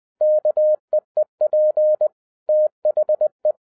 Morse code, heard as a keyed tone, about 20 wpm, 610 Hz. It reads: KEEP THE
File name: 2011-10-mp3 / n13-TIPS FROM MY 34 YEARS OF EXPERIENCE 20wpm